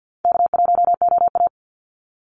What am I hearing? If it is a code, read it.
Morse code, heard as a keyed tone, about 52 wpm, 700 Hz. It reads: K1OA
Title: Call Signs Repeated 2x - 50wpm